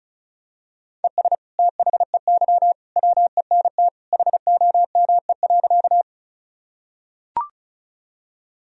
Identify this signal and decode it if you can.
Morse code, heard as a keyed tone, about 36 wpm, 700 Hz. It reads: ES THEY WENT HOME.
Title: Sentences from Top 200 Words and Abbreviations 35wpm